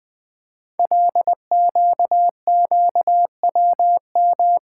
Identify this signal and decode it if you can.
Morse code, heard as a keyed tone, about 20 wpm, 700 Hz. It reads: LQQWM
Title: Code Groups - 5 Letters - 20wpm